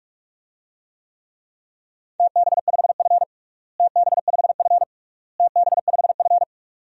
Morse code, heard as a keyed tone, about 46 wpm, 700 Hz. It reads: TB5F TB5F TB5F
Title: DXCC-100 Call Signs - 4 Characters - Repeated 3x - 45wpm